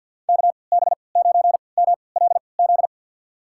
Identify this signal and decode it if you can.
Morse code, heard as a keyed tone, about 47 wpm, spaced as for 28 wpm, 700 Hz. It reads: KB9MLZ
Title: USA Call Signs - 6 Characters - 3x Character Spacing - 50wpm